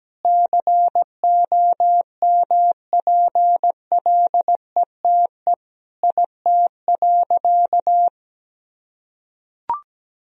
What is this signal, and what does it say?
Morse code, heard as a keyed tone, about 17 wpm, 700 Hz. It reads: COMPLETE IT.
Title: Sentences from Top 1000 Words - Review Part 3 - 17wpm